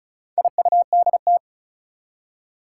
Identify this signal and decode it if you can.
Morse code, heard as a keyed tone, about 36 wpm, 700 Hz. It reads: IUDT
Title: Four Challenging Letter-Number Combinations 35wpm